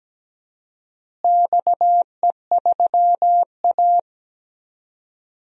Morse code, heard as a keyed tone, about 17 wpm, 700 Hz. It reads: XE3A
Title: DXCC-100 Call Signs - 4 Characters - Repeated 2x - 17wpm